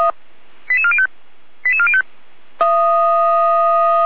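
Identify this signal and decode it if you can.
Five-tone selective call (ZVEI), heard as quick runs of five tones, tones of about 70 ms. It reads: 88405 88405
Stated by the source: Five-tone ZVEI selective call received on an SDR receiver.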